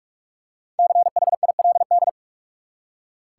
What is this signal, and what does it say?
Morse code, heard as a keyed tone, about 46 wpm, 700 Hz. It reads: CHILD
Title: Five Letter Words - 45wpm